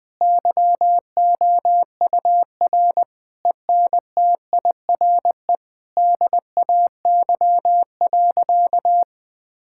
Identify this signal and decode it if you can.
Morse code, heard as a keyed tone, about 20 wpm, 700 Hz. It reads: YOUR ENTIRE DAY.
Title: Difficult Sentences with Hints - 20wpm